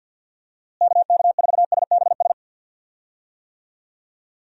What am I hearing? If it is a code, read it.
Morse code, heard as a keyed tone, about 52 wpm, 700 Hz. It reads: KK4SBS
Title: USA Call Signs - 6 Characters - Repeated 3x - 50wpm